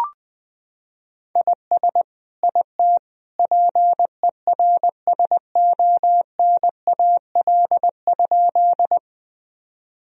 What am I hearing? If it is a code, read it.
Morse code, heard as a keyed tone, about 20 wpm, 700 Hz. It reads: IS IT PERSONAL?
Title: Sentences from Top 1000 Words - Review Part 4 - 20wpm